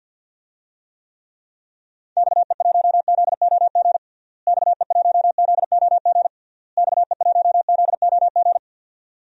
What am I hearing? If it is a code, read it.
Morse code, heard as a keyed tone, about 50 wpm, 700 Hz. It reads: XE1ZOG XE1ZOG XE1ZOG
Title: DXCC-100 Call Signs - 6 Characters - Repeated 3x - 50wpm